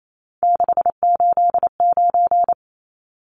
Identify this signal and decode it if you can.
Morse code, heard as a keyed tone, about 28 wpm, 700 Hz. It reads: 689
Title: Numbers - 3-Digit 28wpm